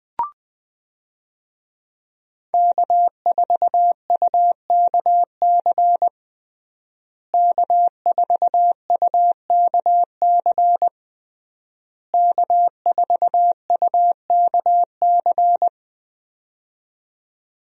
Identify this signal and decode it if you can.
Morse code, heard as a keyed tone, about 21 wpm, 700 Hz. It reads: K4UKC K4UKC K4UKC
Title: USA Call Signs - 5 Characters - Repeated 3x - 20wpm